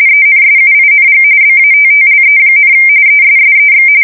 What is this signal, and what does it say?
RTTY, about 45.45 baud. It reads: ENSUING BATTLE, ESSEX
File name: fsk170 / usnavy2